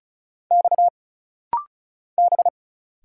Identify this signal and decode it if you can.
Morse code, heard as a keyed tone, about 35 wpm, 700 Hz. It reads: X B
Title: Single Letter-Number Alt - Rapid-Fire 35wpm